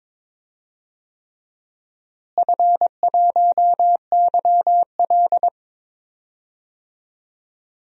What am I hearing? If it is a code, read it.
Morse code, heard as a keyed tone, about 22 wpm, 700 Hz. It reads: F1YL
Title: DXCC-100 Call Signs - 4 Characters - Repeated 3x - 22wpm